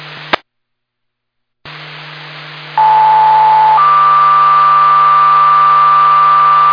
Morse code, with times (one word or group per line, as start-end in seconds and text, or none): none